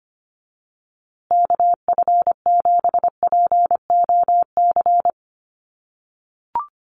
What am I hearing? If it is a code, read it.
Morse code, heard as a keyed tone, about 25 wpm, 700 Hz. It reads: KF7POC
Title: USA Call Signs Repeated 2x 25wpm